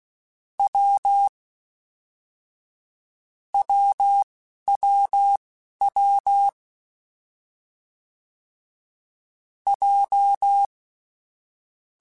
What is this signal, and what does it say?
Morse code, heard as a keyed tone, about 16 wpm, spaced as for 11 wpm, 800 Hz. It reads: W WWW J
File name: Difficult Combos / Difficult Combos-mp3s / AWJ1 @10